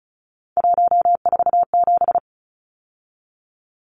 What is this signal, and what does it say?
Morse code, heard as a keyed tone, about 34 wpm, 700 Hz. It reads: 147
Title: Numbers - 3-Digit 35wpm